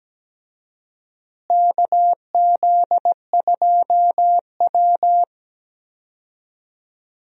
Morse code, heard as a keyed tone, about 17 wpm, 700 Hz. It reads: KZ2W
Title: USA Call Signs - 4 Characters - Repeated 3x - 17wpm